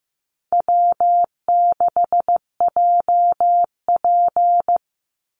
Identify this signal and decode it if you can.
Morse code, heard as a keyed tone, about 15 wpm, 700 Hz. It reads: W6JP
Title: NA SOTA Chaser Callsigns 15wpm